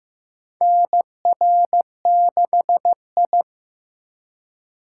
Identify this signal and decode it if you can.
Morse code, heard as a keyed tone, about 15 wpm, 700 Hz. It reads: NR6I
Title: USA Call Signs - 4 Characters - 4x Character Spacing - 15wpm